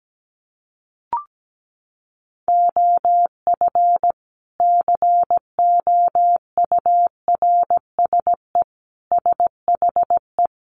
Morse code, heard as a keyed tone, about 17 wpm, 700 Hz. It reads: OF COURSE SHE
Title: Sentences from Top 500 Words 17wpm